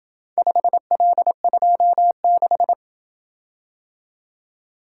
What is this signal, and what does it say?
Morse code, heard as a keyed tone, about 26 wpm, 700 Hz. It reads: 5L26